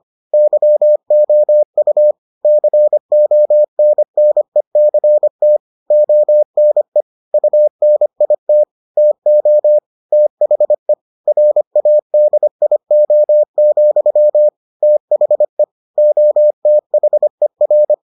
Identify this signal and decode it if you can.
Morse code, heard as a keyed tone, about 25 wpm, 600 Hz. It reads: YOU CONNECT ONE UNIT TO THE RADIO, THE OTHER